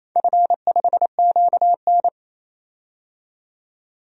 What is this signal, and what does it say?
Morse code, heard as a keyed tone, about 28 wpm, 700 Hz. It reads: F5QN
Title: DXCC-100 Call Signs - 4 Characters - 28wpm